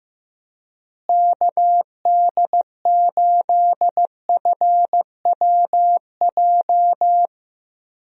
Morse code, heard as a keyed tone, about 15 wpm, 700 Hz. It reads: KD8FWJ
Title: USA Call Signs - 6 Characters - Repeated 3x - 15wpm